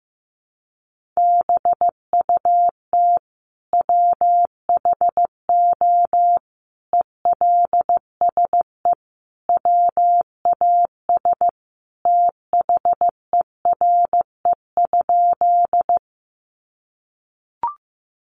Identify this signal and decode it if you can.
Morse code, heard as a keyed tone, about 15 wpm, 700 Hz. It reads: BUT WHO ELSE WAS THERE?
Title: Sentences from Top 800 Words 15wpm